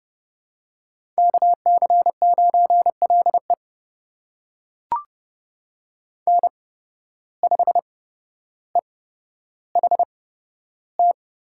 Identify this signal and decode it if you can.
Morse code, heard as a keyed tone, about 29 wpm, spaced as for 13 wpm, 700 Hz. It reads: KC9LE N5EHT
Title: USA Call Signs - 5 Characters - 8x Character Spacing - 30wpm